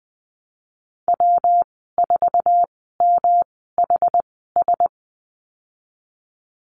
Morse code, heard as a keyed tone, about 20 wpm, spaced as for 15 wpm, 700 Hz. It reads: W4MHS
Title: 122 Lesson 20g - Number 3 - Call Signs 20wpm